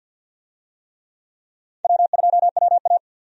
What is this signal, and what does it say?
Morse code, heard as a keyed tone, about 49 wpm, 700 Hz. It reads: W2WA
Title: USA Call Signs - 4 Characters - 50wpm